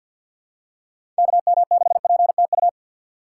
Morse code, heard as a keyed tone, about 49 wpm, 700 Hz. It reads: KM6PTU
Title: USA Call Signs - 6 Characters - 3x Character Spacing - 50wpm